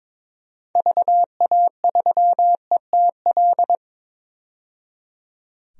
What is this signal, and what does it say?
Morse code, heard as a keyed tone, about 22 wpm, 700 Hz. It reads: VA3ETL
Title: DXCC-100 Call Signs - 6 Characters - 22wpm